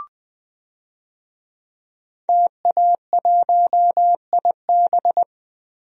Morse code, heard as a keyed tone, about 21 wpm, 700 Hz. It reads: TA1IB
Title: DXCC-100 Call Signs - 5 Characters - 20wpm